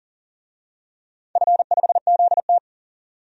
Morse code, heard as a keyed tone, about 40 wpm, 700 Hz. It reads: F5ZT
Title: DXCC-100 Call Signs - 4 Characters - 40wpm